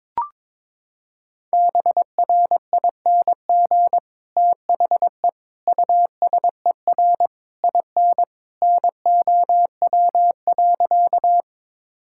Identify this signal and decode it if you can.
Morse code, heard as a keyed tone, about 22 wpm, 700 Hz. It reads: BRING THE USER IN NOW.
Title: Sentences from Top 1000 Words - Review Part 5 - 22wpm